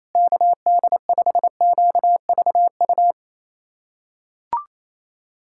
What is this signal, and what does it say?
Morse code, heard as a keyed tone, about 28 wpm, 700 Hz. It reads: KD5QVU
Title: USA Callsign DE Callsign - 6 Characters - 28wpm